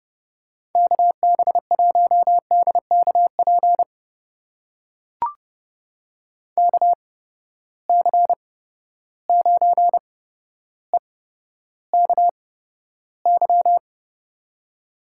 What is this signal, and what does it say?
Morse code, heard as a keyed tone, about 32 wpm, spaced as for 13 wpm, 700 Hz. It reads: KB1DKP KC9EKY
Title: USA Call Signs - 6 Characters - 8x Character Spacing - 30wpm